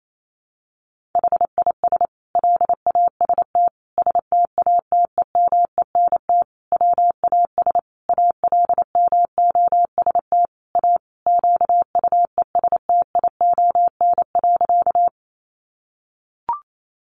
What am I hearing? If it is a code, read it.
Morse code, heard as a keyed tone, about 28 wpm, 700 Hz. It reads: HIS LAST STATEMENT WAS ALMOST A QUESTION.
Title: Sentences from Top 1200 Words 28wpm